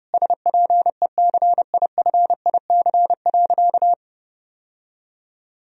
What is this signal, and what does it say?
Morse code, heard as a keyed tone, about 32 wpm, 700 Hz. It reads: SPECIFIC.
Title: Sentences from Top 1000 Words - Review Part 3 - 30wpm